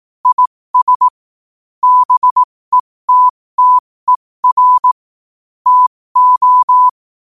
Morse code, heard as a keyed tone, about 18 wpm, spaced as for 15 wpm, 1000 Hz. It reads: IS BETTER TO
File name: qotdfiles / 15wpm